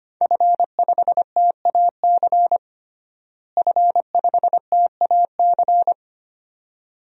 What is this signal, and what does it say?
Morse code, heard as a keyed tone, about 25 wpm, 700 Hz. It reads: F5TAC F5TAC
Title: DXCC-100 Call Signs - 5 Characters - Repeated 2x - 25wpm